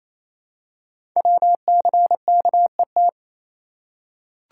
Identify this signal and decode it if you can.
Morse code, heard as a keyed tone, about 27 wpm, 700 Hz. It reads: WCKET
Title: Code Groups - 5 Letters - 28wpm